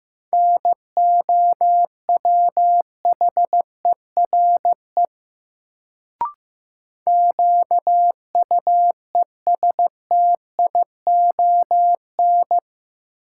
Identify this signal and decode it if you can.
Morse code, heard as a keyed tone, about 15 wpm, 700 Hz. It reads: NOWHERE QUESTION
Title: Sets of 5 Words Encore - New Word Drill - 15wpm